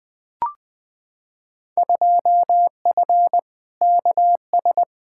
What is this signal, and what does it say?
Morse code, heard as a keyed tone, about 20 wpm, 700 Hz. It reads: 2F KS
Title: Field Day Exchanges - 20wpm